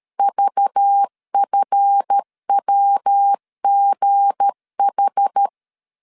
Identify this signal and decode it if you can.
Morse code, heard as a keyed tone, about 13 wpm, 800 Hz. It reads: VFWGH
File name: LEZIONI / 8 YJ45 / 8 YJ 12wpm 2235